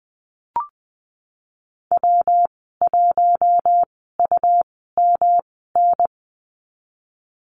Morse code, heard as a keyed tone, about 20 wpm, spaced as for 15 wpm, 700 Hz. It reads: W1UMN
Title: 186 Lesson 30g - Letter b - Call Signs 20wpm